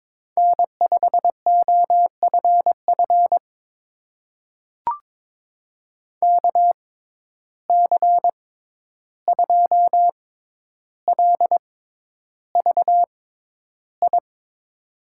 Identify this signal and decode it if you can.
Morse code, heard as a keyed tone, about 22 wpm, spaced as for 11 wpm, 700 Hz. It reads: N5OFF KC2LVI
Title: USA Call Signs - 6x Character Spacing - 22wpm